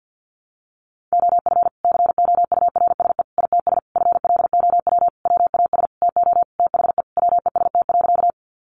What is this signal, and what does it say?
Morse code, heard as a keyed tone, about 50 wpm, 700 Hz. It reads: OF COURSE ITS FLOW WAS TO THE WEST.
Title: Sentences from Top 1200 Words 50wpm